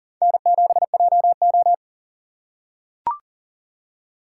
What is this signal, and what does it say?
Morse code, heard as a keyed tone, about 38 wpm, 700 Hz. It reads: N7JO
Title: USA Call Signs - 4 Characters - 4x Character Spacing - 40wpm